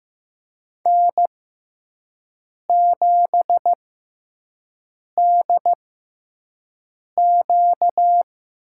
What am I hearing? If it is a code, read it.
Morse code, heard as a keyed tone, about 15 wpm, spaced as for 5 wpm, 700 Hz. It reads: N7DQ
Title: USA Call Signs - 4 Characters - 6x Character Spacing - 15wpm